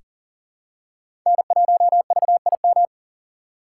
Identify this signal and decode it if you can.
Morse code, heard as a keyed tone, about 40 wpm, 690 Hz. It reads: N1VIM
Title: USA Call Signs - 5 Characters - 3x Character Spacing - 40wpm